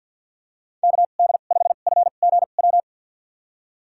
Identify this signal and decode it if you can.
Morse code, heard as a keyed tone, about 50 wpm, spaced as for 36 wpm, 700 Hz. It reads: KD5FGW
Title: USA Call Signs - 2x Character Spacing - 50wpm